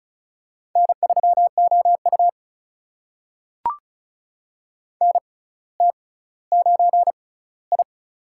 Morse code, heard as a keyed tone, about 36 wpm, spaced as for 18 wpm, 700 Hz. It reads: N3OU NT9I
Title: USA Call Signs - 4 Characters - 6x Character Spacing - 35wpm